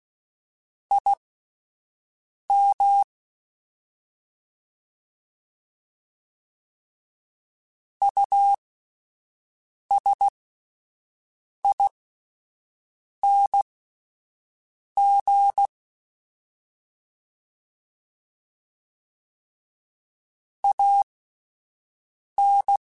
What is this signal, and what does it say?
Morse code, heard as a keyed tone, about 16 wpm, spaced as for 5 wpm, 800 Hz. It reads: IM USING AN
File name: QSOs / QSOs-mp3s / WD0FBT 1@10